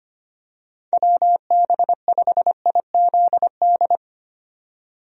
Morse code, heard as a keyed tone, about 25 wpm, 700 Hz. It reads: WB5IZD
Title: USA Call Signs - 6 Characters - 25wpm